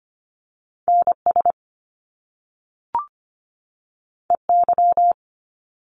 Morse code, heard as a keyed tone, about 25 wpm, 700 Hz. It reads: NS EY
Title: Top Two Letter Combinations in Words 25wpm